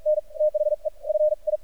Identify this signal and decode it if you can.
Morse code, heard as a keyed tone, about 50 wpm, spaced as for 38 wpm, 610 Hz. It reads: NTHE FI
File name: archived / CRSnet